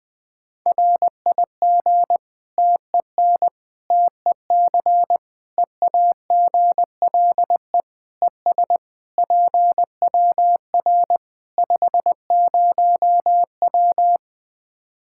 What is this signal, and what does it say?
Morse code, heard as a keyed tone, about 21 wpm, 700 Hz. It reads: RIG TEN TEC EAGLE ES PWR 50W
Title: Practice QSO - Exchange 2 - Info - 20wpm